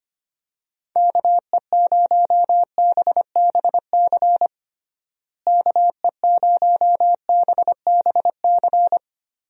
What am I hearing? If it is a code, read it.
Morse code, heard as a keyed tone, about 25 wpm, 700 Hz. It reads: KE0BBC KE0BBC
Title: USA Call Signs - 6 Characters - Repeated 2x - 25wpm